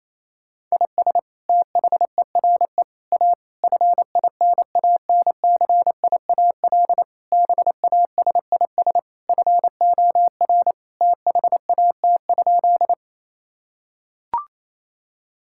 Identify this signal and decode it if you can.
Morse code, heard as a keyed tone, about 28 wpm, 700 Hz. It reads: IS THERE A FINANCIAL BASIS FOR THAT?
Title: Sentences from Top 1000 Words - Review Part 5 - 28wpm